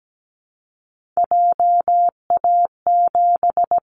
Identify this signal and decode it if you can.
Morse code, heard as a keyed tone, about 17 wpm, 700 Hz. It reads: JA7
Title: Callsign Prefixes for Top 50 DXCC - 17wpm